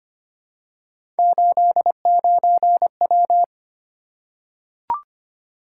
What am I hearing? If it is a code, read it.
Morse code, heard as a keyed tone, about 25 wpm, 700 Hz. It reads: 89W
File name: Three Letter-Number Combinations - 25wpm